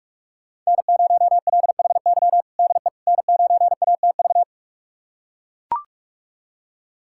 Morse code, heard as a keyed tone, about 42 wpm, 700 Hz. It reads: N0LHY DE N9ATV
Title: USA Callsign DE Callsign - 5 Characters - 45wpm